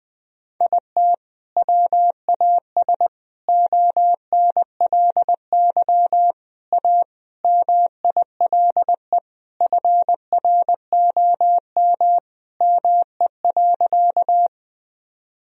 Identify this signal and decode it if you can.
Morse code, heard as a keyed tone, about 21 wpm, 700 Hz. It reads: IT WAS ONLY A MILE FROM ME.
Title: Sentences from Top 1000 Words - Review Part 1 - 20wpm